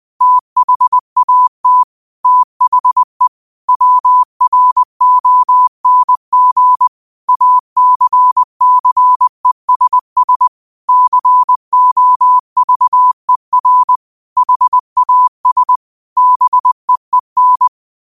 Morse code, heard as a keyed tone, about 20 wpm, 1000 Hz. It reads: THAT THE WRONG ACCESS COVER HAS BEEN
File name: qotdfiles / 20wpm